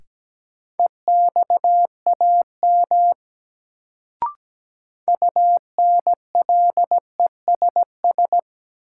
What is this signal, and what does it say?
Morse code, heard as a keyed tone, about 17 wpm, 700 Hz. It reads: EXAM UNLESS
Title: Sentences from Top 1000 Words - Review Part 5 - New Word Drill - 17wpm